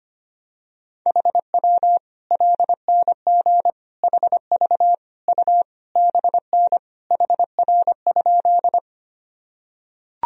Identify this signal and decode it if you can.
Morse code, heard as a keyed tone, about 25 wpm, 700 Hz. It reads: HW LNG HV U BN HR?
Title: Sentences from Top 200 Words and Abbreviations 25wpm